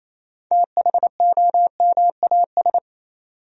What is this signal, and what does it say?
Morse code, heard as a keyed tone, about 28 wpm, 700 Hz. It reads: THOMAS
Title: USA Common Names - 28wpm